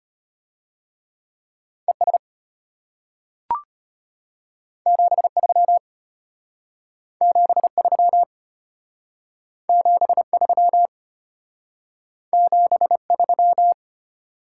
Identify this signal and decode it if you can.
Morse code, heard as a keyed tone, about 32 wpm, 700 Hz. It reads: ES 73 73 73 73